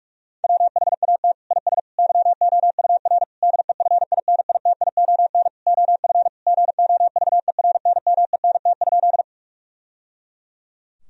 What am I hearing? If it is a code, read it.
Morse code, heard as a keyed tone, about 46 wpm, 700 Hz. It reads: WHAT IS YOUR DEFINITION OF GOVERNMENT?
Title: Sentences from Top 1000 Words - Review Part 4 - 45wpm